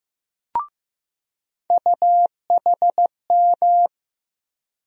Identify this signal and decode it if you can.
Morse code, heard as a keyed tone, about 15 wpm, 700 Hz. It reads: UHM
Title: USA Call Sign Postfixes - 3 Letters 15wpm